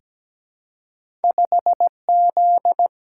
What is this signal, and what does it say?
Morse code, heard as a keyed tone, about 17 wpm, 700 Hz. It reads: 5Z
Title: Single-Letter Single-Number 17wpm